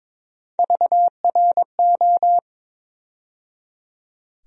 Morse code, heard as a keyed tone, about 22 wpm, 700 Hz. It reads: VRO
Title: USA Call Sign Postfixes - 3 Letters 22wpm